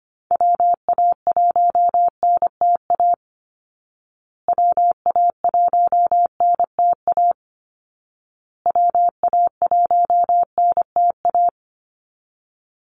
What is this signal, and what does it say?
Morse code, heard as a keyed tone, about 25 wpm, 700 Hz. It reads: WA1NTA WA1NTA WA1NTA
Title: Call Signs Repeated 2x - 25wpm